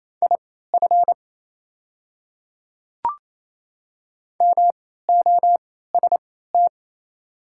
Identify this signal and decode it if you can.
Morse code, heard as a keyed tone, about 28 wpm, spaced as for 16 wpm, 700 Hz. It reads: IF MOST
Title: 3x Character Spacing - Top 100 Words - 28wpm